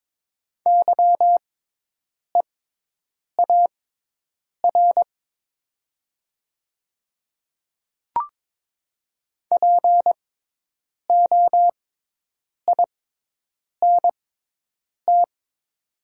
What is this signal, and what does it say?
Morse code, heard as a keyed tone, about 22 wpm, spaced as for 8 wpm, 700 Hz. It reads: YEAR POINT
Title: 6x Character Spacing - Top 500 Words - 22wpm